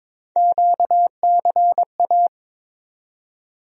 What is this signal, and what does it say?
Morse code, heard as a keyed tone, about 22 wpm, 700 Hz. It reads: QCA
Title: USA Call Sign Postfixes - 3 Letters 22wpm